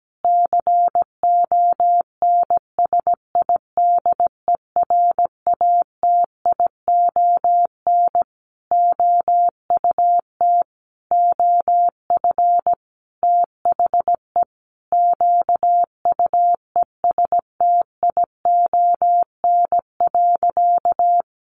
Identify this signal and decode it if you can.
Morse code, heard as a keyed tone, about 17 wpm, 700 Hz. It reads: CONSIDERATION OUT OF THE QUESTION.
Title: Sentences from Top 1500 Words 17wpm